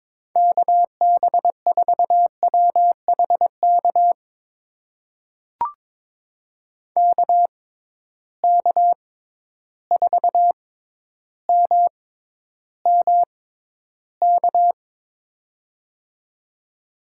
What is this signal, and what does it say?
Morse code, heard as a keyed tone, about 22 wpm, spaced as for 11 wpm, 700 Hz. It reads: KB4WHK KK4MMK
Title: USA Call Signs - 6 Characters - 6x Character Spacing - 22wpm